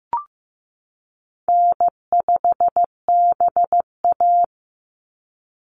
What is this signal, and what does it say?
Morse code, heard as a keyed tone, about 15 wpm, 700 Hz. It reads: N5BA
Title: Easy Call Signs - 15wpm